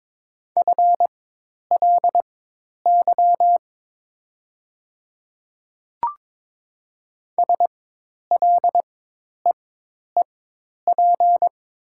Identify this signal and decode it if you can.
Morse code, heard as a keyed tone, about 22 wpm, spaced as for 10 wpm, 700 Hz. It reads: FLY SLEEP
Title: 4x Character Spacing - Top 500 Words - 22wpm